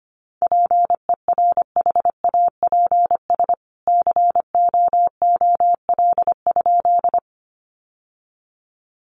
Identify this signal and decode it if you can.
Morse code, heard as a keyed tone, about 25 wpm, 700 Hz. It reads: PERHAPS COOL?
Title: Sentences from Top 1200 Words 25wpm